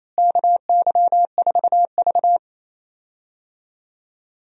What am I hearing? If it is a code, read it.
Morse code, heard as a keyed tone, about 28 wpm, 700 Hz. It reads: KY4V
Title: USA Call Signs - 4 Characters - Repeated 2x - 28wpm